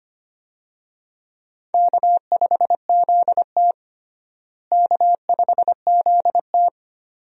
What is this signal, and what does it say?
Morse code, heard as a keyed tone, about 25 wpm, 700 Hz. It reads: K5ZT K5ZT
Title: USA Call Signs - 4 Characters - Repeated 2x - 25wpm